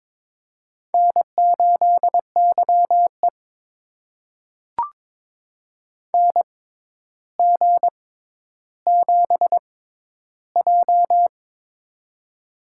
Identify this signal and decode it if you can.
Morse code, heard as a keyed tone, about 22 wpm, spaced as for 11 wpm, 700 Hz. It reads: N8YE NG7J
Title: USA Call Signs - 4 Characters - 6x Character Spacing - 22wpm